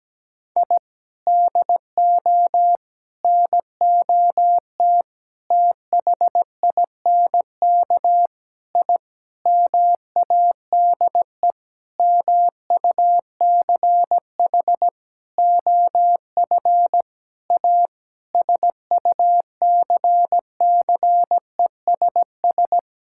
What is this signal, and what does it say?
Morse code, heard as a keyed tone, about 17 wpm, 700 Hz. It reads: I DO NOT THINK I MADE MUCH OF A SUCCESS
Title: Sentences from Top 1000 Words - Review Part 6 - 17wpm